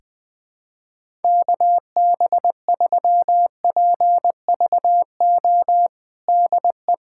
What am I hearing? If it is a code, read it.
Morse code, heard as a keyed tone, about 20 wpm, 700 Hz. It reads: KB3PVO DE
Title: USA Callsign DE Callsign - 6 Characters - 20wpm